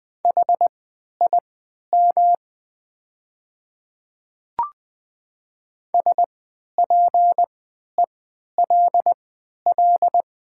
Morse code, heard as a keyed tone, about 20 wpm, spaced as for 11 wpm, 700 Hz. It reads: HIM SPELL
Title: 3x Character Spacing - Top 300 Words - 20wpm